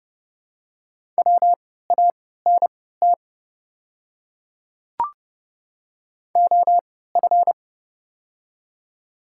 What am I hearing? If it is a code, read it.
Morse code, heard as a keyed tone, about 28 wpm, spaced as for 17 wpm, 690 Hz. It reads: WANT OF
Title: 3x Character Spacing - Top 300 Words - 30wpm